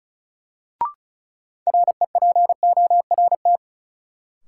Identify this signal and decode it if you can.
Morse code, heard as a keyed tone, about 36 wpm, 700 Hz. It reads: REPORT
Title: Sentences from Top 900 Words - New Word Drill - 35wpm